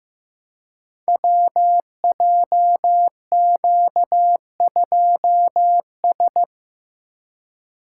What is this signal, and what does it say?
Morse code, heard as a keyed tone, about 15 wpm, 700 Hz. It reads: WJQ2S
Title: Code Groups - 5 Letter:Numbers - 15wpm